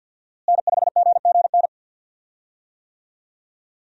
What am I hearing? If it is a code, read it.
Morse code, heard as a keyed tone, about 52 wpm, 700 Hz. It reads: N5GGN
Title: USA Call Signs - 5 Characters - Repeated 3x - 50wpm